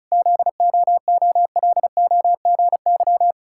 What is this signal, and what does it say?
Morse code, heard as a keyed tone, about 36 wpm, 700 Hz. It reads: ZOOLOGY
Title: Pangram Challenge v2 - 35wpm